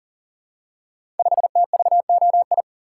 Morse code, heard as a keyed tone, about 41 wpm, 690 Hz. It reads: 5TVOI
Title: Code Groups - 5 Letter:Numbers - 40wpm